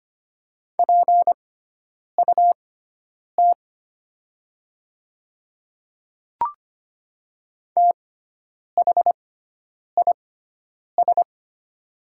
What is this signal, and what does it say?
Morse code, heard as a keyed tone, about 25 wpm, spaced as for 9 wpm, 700 Hz. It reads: PUT THIS
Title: 6x Character Spacing - Top 200 Words - 25wpm